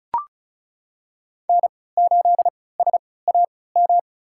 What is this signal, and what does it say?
Morse code, heard as a keyed tone, about 36 wpm, spaced as for 20 wpm, 700 Hz. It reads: N8SAM
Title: USA Call Signs - 5 Characters - 3x Character Spacing - 35wpm